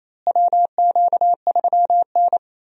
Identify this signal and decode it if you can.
Morse code, heard as a keyed tone, about 28 wpm, 700 Hz. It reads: WQ3N
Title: USA Call Signs - 4 Characters - Repeated 3x - 28wpm